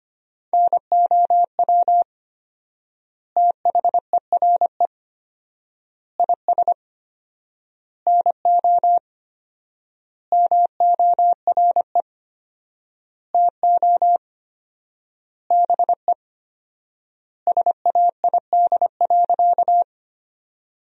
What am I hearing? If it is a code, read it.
Morse code, heard as a keyed tone, about 25 wpm, 700 Hz. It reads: NOW THERE IS NO MORE TO BE SAID.
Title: 4x Word-Spacing - Sentences from Top 100 Words - 25wpm